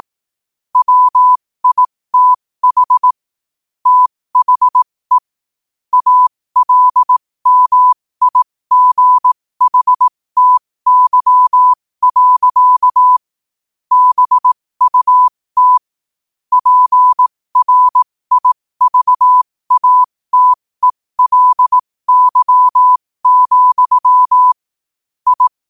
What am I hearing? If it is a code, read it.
Morse code, heard as a keyed tone, about 18 wpm, spaced as for 15 wpm, 1000 Hz. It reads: WITH THE ALMIGHTY. BUT PRIVATELY, I
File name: qotdfiles / 15wpm